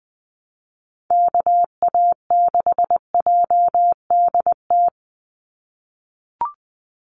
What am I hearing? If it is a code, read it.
Morse code, heard as a keyed tone, about 20 wpm, 700 Hz. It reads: KA6JDT
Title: USA Call Signs 20wpm